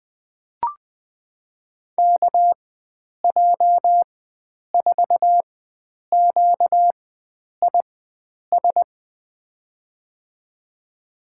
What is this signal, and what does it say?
Morse code, heard as a keyed tone, about 20 wpm, spaced as for 9 wpm, 700 Hz. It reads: KJ4QIS
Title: USA Call Signs - 6 Characters - 4x Character Spacing - 20wpm